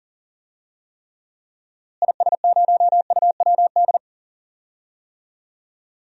Morse code, heard as a keyed tone, about 40 wpm, 700 Hz. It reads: IS0UWD
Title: DXCC-100 Call Signs - 6 Characters - Repeated 2x - 40wpm